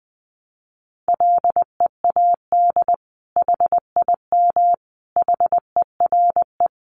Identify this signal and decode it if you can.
Morse code, heard as a keyed tone, about 20 wpm, 700 Hz. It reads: LEAD HIM HERE
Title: Sentences from Top 1200 Words 20wpm